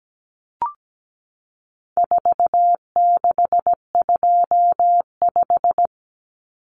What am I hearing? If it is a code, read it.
Morse code, heard as a keyed tone, about 17 wpm, 700 Hz. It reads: 4625
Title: Common Cut Numbers - 4-Digit 17wpm